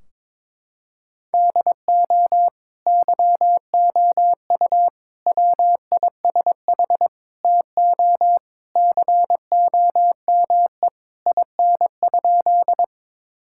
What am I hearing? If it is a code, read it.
Morse code, heard as a keyed tone, about 22 wpm, 700 Hz. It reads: DO YOU WISH TO COME IN?
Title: Sentences from Top 1600 Words 22wpm